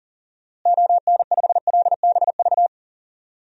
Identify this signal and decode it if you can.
Morse code, heard as a keyed tone, about 40 wpm, 700 Hz. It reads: ON5LBV
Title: DXCC-100 Call Signs - 6 Characters - Repeated 2x - 40wpm